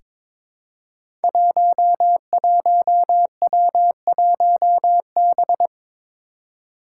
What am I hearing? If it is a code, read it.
Morse code, heard as a keyed tone, about 21 wpm, 700 Hz. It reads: 11W1B
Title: Five Challenging Letter-Number Combinations 22wpm